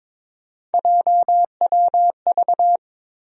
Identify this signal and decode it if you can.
Morse code, heard as a keyed tone, about 22 wpm, 700 Hz. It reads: JWV
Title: USA Call Sign Postfixes - 3 Letters 22wpm